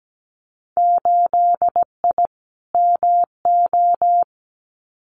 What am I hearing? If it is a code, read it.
Morse code, heard as a keyed tone, about 17 wpm, 700 Hz. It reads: 8I MO
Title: Winter Field Day Exchanges Encore 17wpm